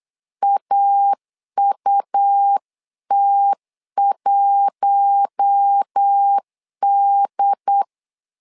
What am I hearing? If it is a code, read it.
Morse code, heard as a keyed tone, about 8 wpm, 800 Hz. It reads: AUT1D
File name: LEZIONI / 4 Q1S 8wpm 2710